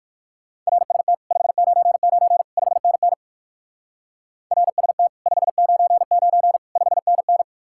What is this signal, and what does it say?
Morse code, heard as a keyed tone, about 57 wpm, 700 Hz. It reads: RST 599 5NN RST 599 5NN